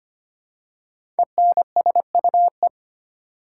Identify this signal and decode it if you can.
Morse code, heard as a keyed tone, about 25 wpm, 700 Hz. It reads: ENSUE
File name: 5 Letter Words - First 3 Letters Given - 25wpm